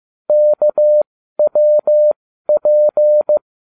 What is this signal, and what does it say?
Morse code, heard as a keyed tone, about 15 wpm, spaced as for 12 wpm, 600 Hz. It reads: KWP